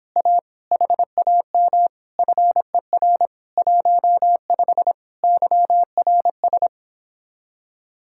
Morse code, heard as a keyed tone, about 26 wpm, 700 Hz. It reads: A HAM FER 15 YRS